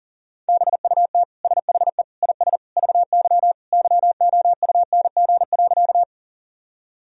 Morse code, heard as a keyed tone, about 40 wpm, 700 Hz. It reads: BUT SHE IS VY YOUNG.
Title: Sentences from Top 400 Words and Abbreviations 40wpm